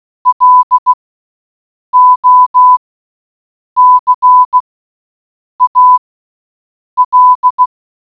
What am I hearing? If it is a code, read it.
Morse code, heard as a keyed tone, about 16 wpm, spaced as for 7 wpm, 1000 Hz. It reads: LOCAL